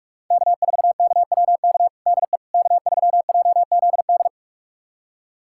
Morse code, heard as a keyed tone, about 46 wpm, 700 Hz. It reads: K4KWK DE K3JZD
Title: USA Callsign DE Callsign - 5 Characters - 45wpm